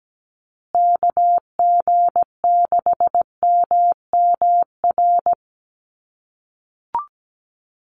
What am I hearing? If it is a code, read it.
Morse code, heard as a keyed tone, about 17 wpm, 700 Hz. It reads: KG6MMR
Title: USA Call Signs Repeated 3x 17wpm